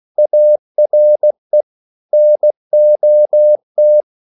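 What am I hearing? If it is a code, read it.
Morse code, heard as a keyed tone, about 16 wpm, 600 Hz. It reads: ARE NOT
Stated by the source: English quotations sent in CW as practice at 16 words per minute.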